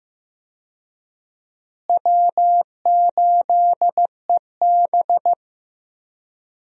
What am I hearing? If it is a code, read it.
Morse code, heard as a keyed tone, about 15 wpm, 700 Hz. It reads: W8EB
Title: USA Call Signs - 4 Characters - Repeated 3x - 15wpm